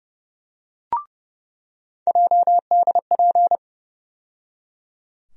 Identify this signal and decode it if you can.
Morse code, heard as a keyed tone, about 31 wpm, 700 Hz. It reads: JDP
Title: USA Call Sign Postfixes - 3 Letters 30wpm